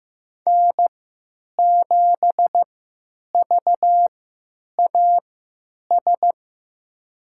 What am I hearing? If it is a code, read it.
Morse code, heard as a keyed tone, about 15 wpm, spaced as for 9 wpm, 700 Hz. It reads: N7VAS
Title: USA Call Signs - 5 Characters - 3x Character Spacing - 15wpm